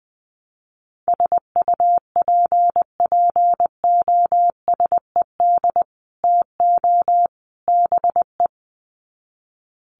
Morse code, heard as a keyed tone, about 19 wpm, 700 Hz. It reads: SUPPOSED TO BE
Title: Sets of 3 Words 20wpm